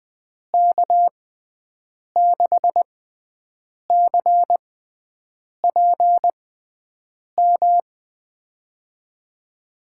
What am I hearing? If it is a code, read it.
Morse code, heard as a keyed tone, about 20 wpm, spaced as for 7 wpm, 700 Hz. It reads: K6CPM
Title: USA Call Signs - 6 Characters - 6x Character Spacing - 20wpm